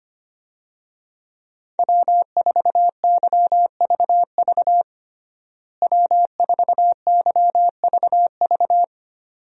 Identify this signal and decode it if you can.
Morse code, heard as a keyed tone, about 25 wpm, 700 Hz. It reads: W4YVV W4YVV
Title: USA Call Signs - 5 Characters - Repeated 2x - 25wpm